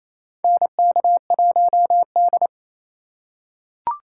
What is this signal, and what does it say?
Morse code, heard as a keyed tone, about 28 wpm, 700 Hz. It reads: NK1D
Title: USA Call Signs - 4 Characters - 2x Character Spacing - 28wpm